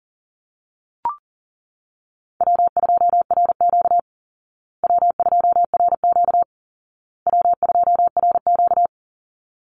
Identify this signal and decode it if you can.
Morse code, heard as a keyed tone, about 40 wpm, 700 Hz. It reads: W2RQ W2RQ W2RQ
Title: Call Signs Repeated 2x - 40wpm